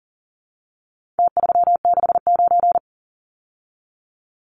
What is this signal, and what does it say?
Morse code, heard as a keyed tone, about 40 wpm, 700 Hz. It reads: T369
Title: Numbers - Contest Style - 4-Digit 40wpm